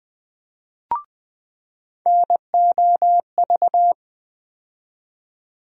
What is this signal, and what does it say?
Morse code, heard as a keyed tone, about 20 wpm, 700 Hz. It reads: NOV
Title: USA Call Sign Postfixes - 3 Letters 20wpm